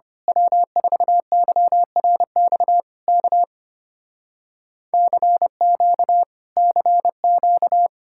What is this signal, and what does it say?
Morse code, heard as a keyed tone, about 24 wpm, 700 Hz. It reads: W4YRX K CQ CQ
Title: Speed-Racing Common QSO Phrases Encore - 25wpm